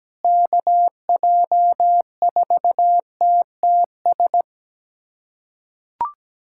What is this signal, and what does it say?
Morse code, heard as a keyed tone, about 17 wpm, 700 Hz. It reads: KJ4TTS
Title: USA Callsign DE Callsign - 6 Characters - 17wpm